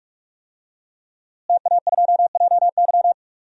Morse code, heard as a keyed tone, about 46 wpm, 700 Hz. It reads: TA2JY
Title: DXCC-100 Call Signs - 5 Characters - Repeated 2x - 45wpm